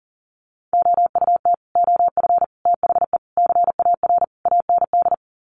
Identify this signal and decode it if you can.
Morse code, heard as a keyed tone, about 40 wpm, 700 Hz. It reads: OUT OF THE CAR AND
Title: Sets of 5 Words Encore 40wpm